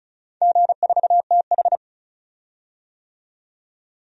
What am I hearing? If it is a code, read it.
Morse code, heard as a keyed tone, about 36 wpm, 700 Hz. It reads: G4TH
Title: DXCC-100 Call Signs - 4 Characters - 35wpm